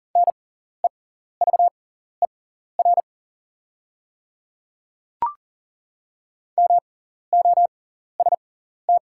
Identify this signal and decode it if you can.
Morse code, heard as a keyed tone, about 40 wpm, spaced as for 14 wpm, 690 Hz. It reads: NEVER MOST
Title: 6x Character Spacing - Top 300 Words - 40wpm